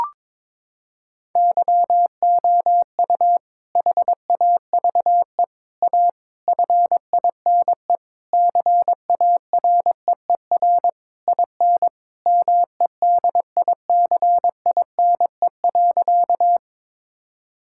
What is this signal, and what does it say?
Morse code, heard as a keyed tone, about 22 wpm, 700 Hz. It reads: YOU HAVE A FINE CAREER IN MEDICINE.
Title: Sentences from Top 1000 Words - Review Part 6 - 22wpm